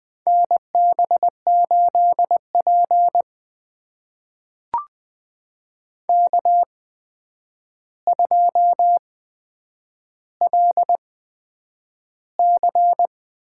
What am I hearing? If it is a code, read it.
Morse code, heard as a keyed tone, about 20 wpm, spaced as for 9 wpm, 700 Hz. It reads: NB8P K2LC
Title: USA Call Signs - 4 Characters - 8x Character Spacing - 20wpm